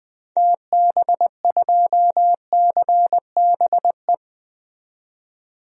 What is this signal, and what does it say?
Morse code, heard as a keyed tone, about 20 wpm, 700 Hz. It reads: TB2CBE
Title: DXCC-100 Call Signs - 6 Characters - 20wpm